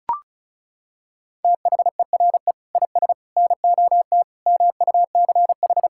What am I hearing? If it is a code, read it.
Morse code, heard as a keyed tone, about 36 wpm, 700 Hz. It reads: THERE IS NOT MUCH
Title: Sentences from Top 1600 Words 35wpm